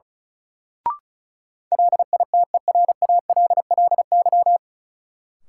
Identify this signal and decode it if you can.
Morse code, heard as a keyed tone, about 34 wpm, 700 Hz. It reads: LITERALLY
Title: Sentences from Top 1400 Words - New Word Drill Alt - 35wpm